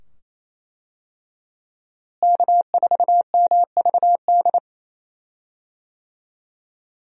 Morse code, heard as a keyed tone, about 27 wpm, 700 Hz. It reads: K4MVD
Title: USA Call Signs - 5 Characters - Repeated 2x - 28wpm